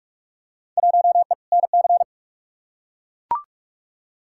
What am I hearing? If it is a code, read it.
Morse code, heard as a keyed tone, about 42 wpm, 700 Hz. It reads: 1E NC
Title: Field Day Exchanges - 45wpm